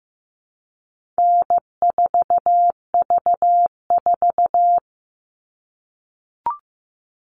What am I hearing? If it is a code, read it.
Morse code, heard as a keyed tone, about 15 wpm, 700 Hz. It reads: N4V4
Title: All Cut Numbers - 4-Digit 15wpm